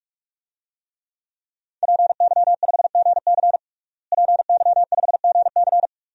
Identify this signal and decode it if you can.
Morse code, heard as a keyed tone, about 46 wpm, 700 Hz. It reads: PY5GC PY5GC
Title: DXCC-100 Call Signs - 5 Characters - Repeated 2x - 45wpm